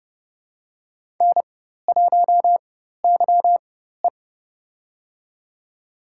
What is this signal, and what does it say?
Morse code, heard as a keyed tone, about 30 wpm, spaced as for 14 wpm, 700 Hz. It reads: N1YE
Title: USA Call Signs - 4x Character Spacing - 30wpm